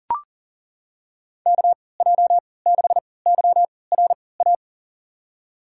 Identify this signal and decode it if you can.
Morse code, heard as a keyed tone, about 40 wpm, spaced as for 23 wpm, 700 Hz. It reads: KJ6YRA
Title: USA Call Signs - 6 Characters - 3x Character Spacing - 40wpm